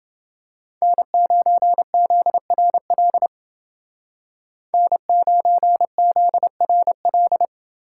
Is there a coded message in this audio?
Yes: Morse code from 0.8 to 7.5 s, heard as a keyed tone, about 28 wpm, 700 Hz. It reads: N9ZRL N9ZRL